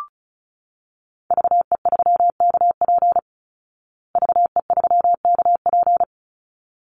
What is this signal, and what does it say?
Morse code, heard as a keyed tone, about 34 wpm, 700 Hz. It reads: VE3KP VE3KP
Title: Call Signs Repeated 3x - 35wpm